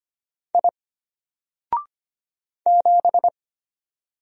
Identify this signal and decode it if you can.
Morse code, heard as a keyed tone, about 25 wpm, 700 Hz. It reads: I 7
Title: Single Letter-Number Alt 25wpm